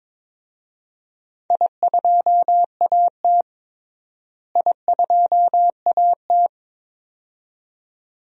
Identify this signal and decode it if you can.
Morse code, heard as a keyed tone, about 22 wpm, 700 Hz. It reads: I2AT I2AT
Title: DXCC-100 Call Signs - 4 Characters - Repeated 2x - 22wpm